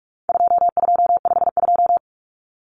Morse code, heard as a keyed tone, about 44 wpm, 700 Hz. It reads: 2252
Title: Common Cut Numbers - 4-Digit 45wpm